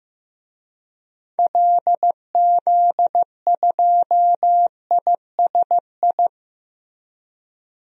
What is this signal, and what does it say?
Morse code, heard as a keyed tone, about 15 wpm, 700 Hz. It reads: LZ2ISI
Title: DXCC-100 Call Signs - 6 Characters - Repeated 3x - 15wpm